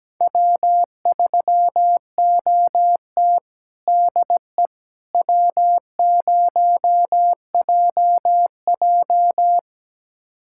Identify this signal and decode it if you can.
Morse code, heard as a keyed tone, about 17 wpm, 700 Hz. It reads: W3OT DE W0JJ
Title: USA Callsign DE Callsign - 4 Characters - 17wpm